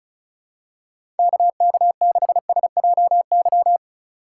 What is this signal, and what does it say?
Morse code, heard as a keyed tone, about 34 wpm, 700 Hz. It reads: KK6SJY
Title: USA Call Signs - 6 Characters - 2x Character Spacing - 35wpm